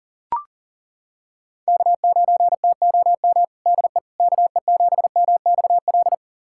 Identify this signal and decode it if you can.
Morse code, heard as a keyed tone, about 37 wpm, 700 Hz. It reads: K9TOM DE KE7MXL
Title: USA Callsign DE Callsign 40wpm